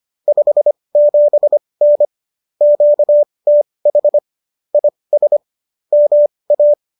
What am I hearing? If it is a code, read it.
Morse code, heard as a keyed tone, about 25 wpm, spaced as for 20 wpm, 590 Hz. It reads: 57N QTH IS MA
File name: qso201 20